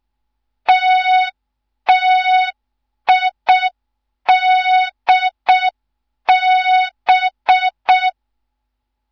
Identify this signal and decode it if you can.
Morse code, heard as a keyed tone, about 6 wpm, 3700 Hz. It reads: TTIDB
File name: Morze-HA2MN / hang / 15 GYAKORLAT